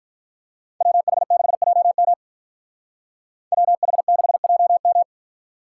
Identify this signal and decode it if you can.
Morse code, heard as a keyed tone, about 48 wpm, 700 Hz. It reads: WH6JM WH6JM